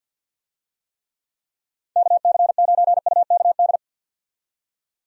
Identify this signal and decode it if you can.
Morse code, heard as a keyed tone, about 50 wpm, 700 Hz. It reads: KC9UKD
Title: USA Call Signs - 6 Characters - 50wpm